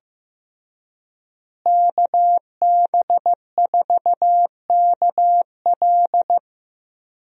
Morse code, heard as a keyed tone, about 15 wpm, 700 Hz. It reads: KB4KL
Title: USA Call Signs - 5 Characters - Repeated 3x - 15wpm